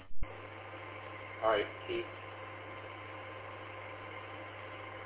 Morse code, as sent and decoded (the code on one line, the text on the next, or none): none